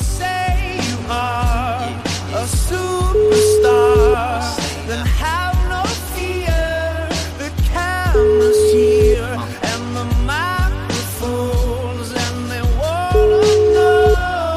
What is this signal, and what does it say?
Ringback tone (Europe, 425 Hz), with noise.